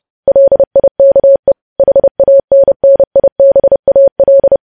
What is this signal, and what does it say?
Morse code, heard as a keyed tone, about 30 wpm, 550 Hz. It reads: LIKE HANNIBAL
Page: news headlines in morse code at 30 wpm